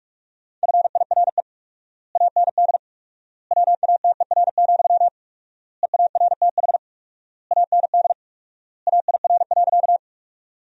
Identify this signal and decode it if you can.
Morse code, heard as a keyed tone, about 46 wpm, 700 Hz. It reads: FIRE AND WATER, EARTH AND AIR.
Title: 4x Word-Spacing - Sentences from Top 400 Words - 45wpm